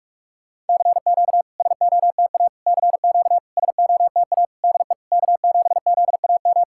Morse code, heard as a keyed tone, about 44 wpm, 700 Hz. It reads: CQ SOTA CQ SOTA DE K7ZAM
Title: SOTA QSO Practice 45wpm